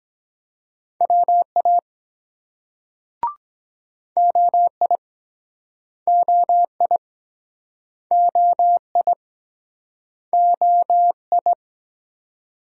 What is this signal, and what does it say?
Morse code, heard as a keyed tone, about 24 wpm, 700 Hz. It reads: WA OI OI OI OI